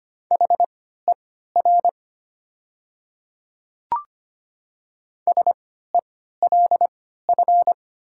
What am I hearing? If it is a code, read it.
Morse code, heard as a keyed tone, about 25 wpm, spaced as for 14 wpm, 700 Hz. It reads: HER SELF
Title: 3x Character Spacing - Top 200 Words - 25wpm